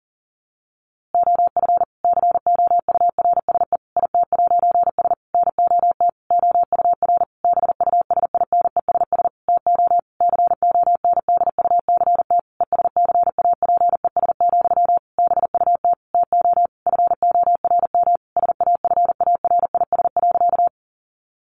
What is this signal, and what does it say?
Morse code, heard as a keyed tone, about 40 wpm, 700 Hz. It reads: OF COURSE IT'S NOT OUR BUSINESS TO CONDUCT ESCAPES, BUT TO FORM SAFARIS.
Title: A Sound Of Thunder 40wpm